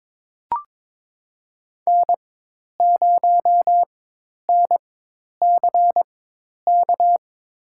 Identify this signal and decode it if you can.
Morse code, heard as a keyed tone, about 22 wpm, spaced as for 10 wpm, 700 Hz. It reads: N0NCK
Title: USA Call Signs - 5 Characters - 4x Character Spacing - 22wpm